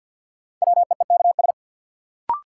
Morse code, heard as a keyed tone, about 53 wpm, 700 Hz. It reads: WEEKS
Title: Sets of 5 Words - New Word Drill - 50wpm